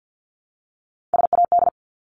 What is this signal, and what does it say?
Morse code, heard as a keyed tone, about 50 wpm, 700 Hz. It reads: SAD
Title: Top 1501-1750 Words 50wpm